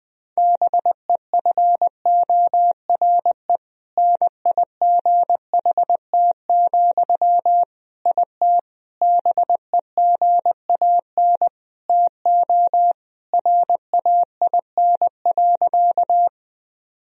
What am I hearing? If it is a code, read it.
Morse code, heard as a keyed tone, about 21 wpm, 700 Hz. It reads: BEFORE NIGHT, IT BEGAN TO RAIN.
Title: Sentences from Top 1000 Words - Review Part 1 - 20wpm